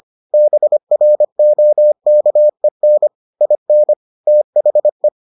Morse code, heard as a keyed tone, about 25 wpm, 600 Hz. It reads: BROKEN IN THE